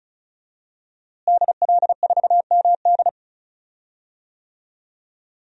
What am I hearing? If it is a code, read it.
Morse code, heard as a keyed tone, about 36 wpm, 700 Hz. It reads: DL4MD
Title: DXCC-100 Call Signs - 5 Characters - Repeated 3x - 35wpm